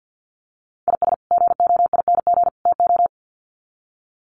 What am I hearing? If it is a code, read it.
Morse code, heard as a keyed tone, about 48 wpm, 700 Hz. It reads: IS GOING TO
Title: Sets of 3 Words 50wpm